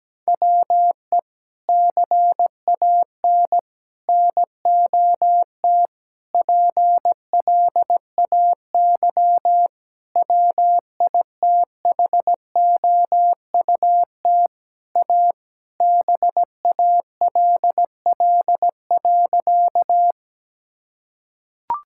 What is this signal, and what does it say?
Morse code, heard as a keyed tone, about 17 wpm, 700 Hz. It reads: WE CAN NOT PLAY WITHOUT A BALL.
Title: Sentences from Top 1000 Words - Review Part 6 - 17wpm